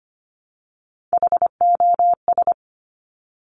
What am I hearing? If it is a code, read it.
Morse code, heard as a keyed tone, about 25 wpm, 700 Hz. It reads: HOS
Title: Top Three Letter Combinations in Words 25wpm